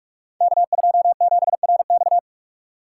Morse code, heard as a keyed tone, about 44 wpm, 700 Hz. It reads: K27RX
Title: Code Groups - 5 Letter:Numbers - 45wpm